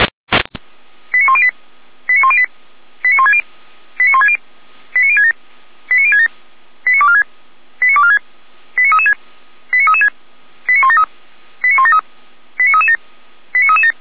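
Five-tone selective call (ZVEI), heard as quick runs of five tones, tones of about 70 ms. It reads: 89108 89108 89160 89160 89067 89067 89246 89246 89336 89336 89173 89173 89337 89337
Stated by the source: ZVEI five-tone call 89072 received via SDR, Feuerwehr Orlamünde fire brigade.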